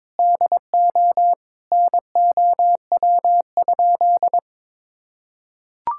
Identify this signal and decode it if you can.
Morse code, heard as a keyed tone, about 22 wpm, 700 Hz. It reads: DO NOW?
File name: practice / 22wpm / Sentences with 3-Word Phrases - 22wpm